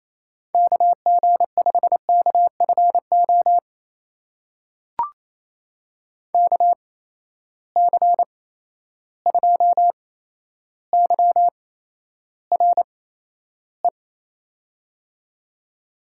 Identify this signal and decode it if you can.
Morse code, heard as a keyed tone, about 28 wpm, spaced as for 12 wpm, 700 Hz. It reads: KG5KFO KC2YRE
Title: USA Call Signs - 6 Characters - 8x Character Spacing - 28wpm